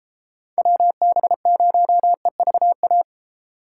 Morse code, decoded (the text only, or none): WB0EVA